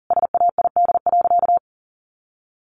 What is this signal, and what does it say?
Morse code, heard as a keyed tone, about 40 wpm, 700 Hz. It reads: SAID.
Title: Sentences from Top 1500 Words 40wpm